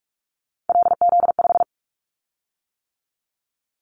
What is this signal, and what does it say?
Morse code, heard as a keyed tone, about 46 wpm, 700 Hz. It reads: LZ5
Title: Callsign Prefixes for Top 50 DXCC - 45wpm